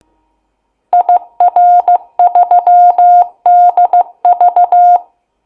Morse code, heard as a keyed tone, about 15 wpm, 710 Hz. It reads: IR3DV
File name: Kennung-Kronplatz-CW